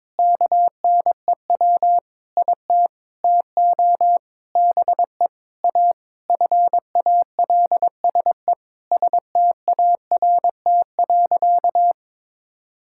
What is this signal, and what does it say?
Morse code, heard as a keyed tone, about 21 wpm, 700 Hz. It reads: KNEW IT TO BE A FALSE START.
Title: Sentences from Top 1600 Words 22wpm